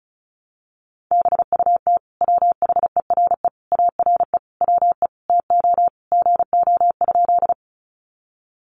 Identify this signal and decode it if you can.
Morse code, heard as a keyed tone, about 34 wpm, 700 Hz. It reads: BUT WHERE ARE WE TO GO?
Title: Sentences from Top 200 Words 35wpm